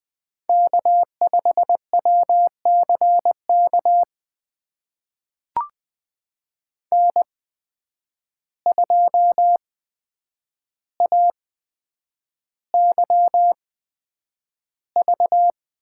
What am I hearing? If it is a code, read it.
Morse code, heard as a keyed tone, about 20 wpm, spaced as for 9 wpm, 700 Hz. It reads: K5WCK N2AYV
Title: USA Call Signs - 8x Character Spacing - 20wpm